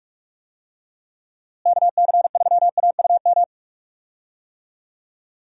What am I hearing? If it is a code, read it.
Morse code, heard as a keyed tone, about 46 wpm, 700 Hz. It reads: KC3AUM
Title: USA Call Signs - 6 Characters - Repeated 2x - 45wpm